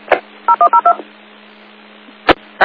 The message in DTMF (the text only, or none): 0102